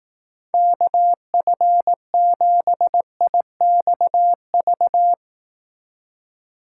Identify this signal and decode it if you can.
Morse code, heard as a keyed tone, about 18 wpm, 700 Hz. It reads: KF7IXV
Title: Speed Racing - USA 2x3 Call Signs 15wpm